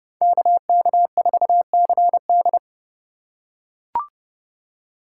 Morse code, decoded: KK4CD

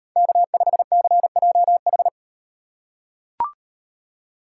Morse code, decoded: K5CJH

